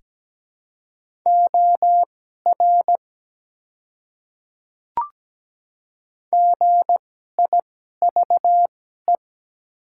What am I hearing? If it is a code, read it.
Morse code, heard as a keyed tone, about 17 wpm, spaced as for 12 wpm, 700 Hz. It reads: OR GIVE
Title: 2x Character Spacing - Top 200 Words - 17wpm